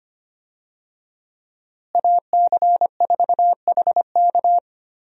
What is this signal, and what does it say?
Morse code, heard as a keyed tone, about 25 wpm, 700 Hz. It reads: AC4HK
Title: USA Call Signs - 5 Characters - Repeated 2x - 25wpm